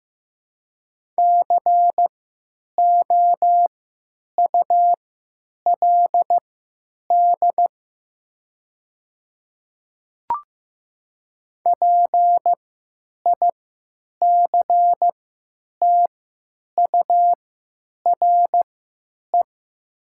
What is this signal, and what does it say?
Morse code, heard as a keyed tone, about 15 wpm, spaced as for 9 wpm, 700 Hz. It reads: COULD PICTURE
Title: 3x Character Spacing - Top 200 Words - 15wpm